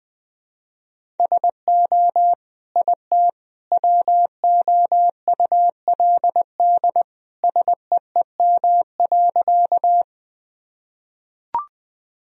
Morse code, heard as a keyed tone, about 20 wpm, 700 Hz. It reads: SO IT WOULD SEEM.
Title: Sentences from Top 1000 Words - Review Part 1 - 20wpm